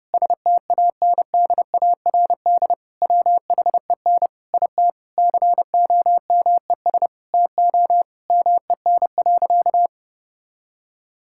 Morse code, heard as a keyed tone, about 32 wpm, 700 Hz. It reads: STANDARD WHEN IT COMES TO MEN.
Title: Sentences from Top 1000 Words - Review Part 6 - 30wpm